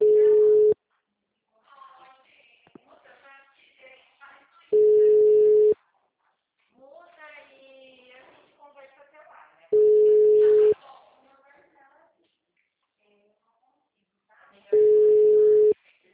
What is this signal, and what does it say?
Ringback tone (Europe, 425 Hz), with speech.
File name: out- -2031- - - .642